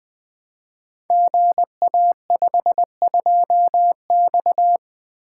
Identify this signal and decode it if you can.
Morse code, heard as a keyed tone, about 20 wpm, 700 Hz. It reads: GA52X
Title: Code Groups - 5 Letter:Numbers - 20wpm